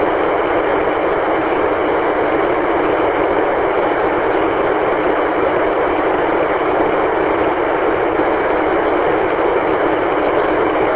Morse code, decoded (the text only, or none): none